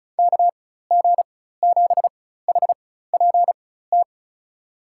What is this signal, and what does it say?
Morse code, heard as a keyed tone, about 34 wpm, spaced as for 16 wpm, 700 Hz. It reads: KG7HPT
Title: USA Call Signs - 6 Characters - 4x Character Spacing - 35wpm